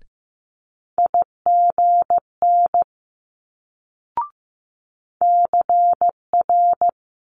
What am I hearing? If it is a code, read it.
Morse code, heard as a keyed tone, about 15 wpm, 700 Hz. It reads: IGN CR
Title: Multi-letter Phonemes Quick Fire 15wpm